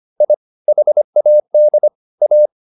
Morse code, heard as a keyed tone, about 25 wpm, 600 Hz. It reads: I HAD A